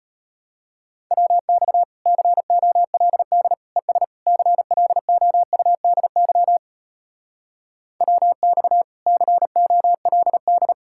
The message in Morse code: WX COLD ES CLOUDY WX COLD

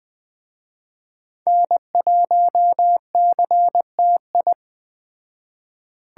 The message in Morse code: N1CTI

